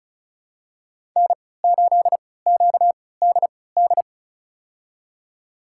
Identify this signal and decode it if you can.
Morse code, heard as a keyed tone, about 34 wpm, spaced as for 20 wpm, 700 Hz. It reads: N8QDD
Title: USA Call Signs - 5 Characters - 3x Character Spacing - 35wpm